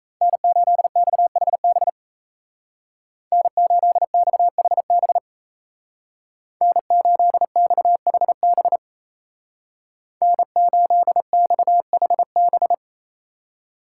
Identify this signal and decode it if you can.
Morse code, heard as a keyed tone, about 33 wpm, 700 Hz. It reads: N8XHB N8XHB N8XHB N8XHB
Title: Speed Racing - USA 1x3 Call Signs 28wpm